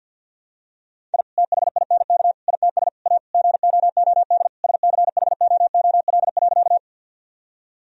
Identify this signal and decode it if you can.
Morse code, heard as a keyed tone, about 48 wpm, 700 Hz. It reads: I THINK ITS A GOOD SCHOOL.
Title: Sentences from Top 1000 Words - Review Part 1 - 50wpm